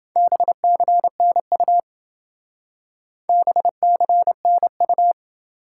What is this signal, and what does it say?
Morse code, heard as a keyed tone, about 28 wpm, 700 Hz. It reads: BCNU BCNU